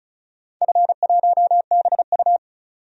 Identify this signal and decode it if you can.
Morse code, heard as a keyed tone, about 36 wpm, 700 Hz. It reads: F1BU
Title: DXCC-100 Call Signs - 4 Characters - 35wpm